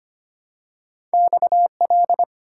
Morse code, heard as a keyed tone, about 25 wpm, 700 Hz. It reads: XL
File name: Two Letter-Number Combinations - 25wpm